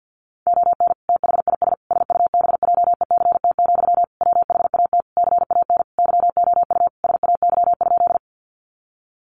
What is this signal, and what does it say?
Morse code, heard as a keyed tone, about 50 wpm, 700 Hz. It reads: ON THIS SUBJECT, WHAT CAN YOU SAY?
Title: Sentences from Top 800 Words 50wpm